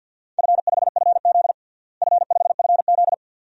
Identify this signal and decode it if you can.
Morse code, heard as a keyed tone, about 52 wpm, 700 Hz. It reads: F5FZ F5FZ
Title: DXCC-100 Call Signs - 4 Characters - Repeated 2x - 50wpm